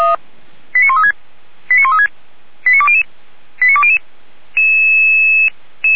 Five-tone selective call (ZVEI), heard as quick runs of five tones, tones of about 70 ms. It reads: 89136 89136 89200 89200